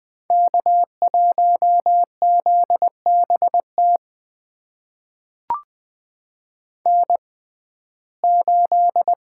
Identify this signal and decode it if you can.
Morse code, heard as a keyed tone, about 20 wpm, spaced as for 14 wpm, 700 Hz. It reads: K1ZBT N8
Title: USA Call Signs - 5 Characters - 6x Character Spacing - 20wpm